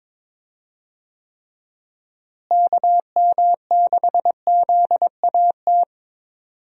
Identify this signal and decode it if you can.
Morse code, heard as a keyed tone, about 22 wpm, 700 Hz. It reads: KM6ZAT
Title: USA Call Signs - 6 Characters - Repeated 2x - 22wpm